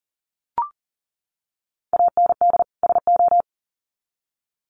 Morse code, heard as a keyed tone, about 40 wpm, 700 Hz. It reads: AND SO
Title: Sets of 2 Words 40wpm